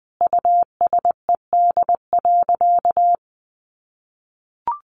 Morse code, heard as a keyed tone, about 20 wpm, 700 Hz. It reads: USED.
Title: Sentences from Top 700 Words 20wpm